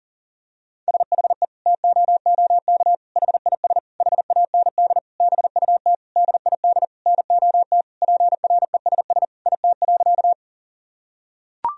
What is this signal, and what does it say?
Morse code, heard as a keyed tone, about 40 wpm, 700 Hz. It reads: SHE TOOK HIS HAND BUT DID NOT PRESS IT.
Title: Sentences from Top 1000 Words - Review Part 1 - 40wpm